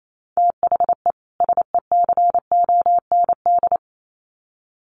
Morse code, heard as a keyed tone, about 28 wpm, 700 Hz. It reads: THE SECOND
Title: Sets of 2 Words 28wpm